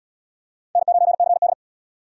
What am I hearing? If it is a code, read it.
Morse code, heard as a keyed tone, about 77 wpm, 700 Hz. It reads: N9CD